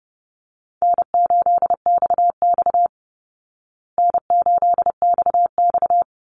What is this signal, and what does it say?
Morse code, heard as a keyed tone, about 30 wpm, 700 Hz. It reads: N8XX N8XX
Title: Easy Call Signs Repeated 2x - 30wpm